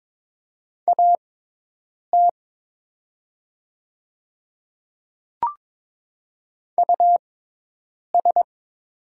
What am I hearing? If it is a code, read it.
Morse code, heard as a keyed tone, about 22 wpm, spaced as for 8 wpm, 700 Hz. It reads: AT US
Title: 6x Character Spacing - Top 300 Words - 22wpm